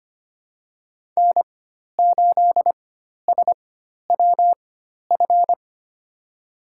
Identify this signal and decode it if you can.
Morse code, heard as a keyed tone, about 25 wpm, spaced as for 12 wpm, 700 Hz. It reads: N8SWF
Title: USA Call Signs - 5 Characters - 4x Character Spacing - 25wpm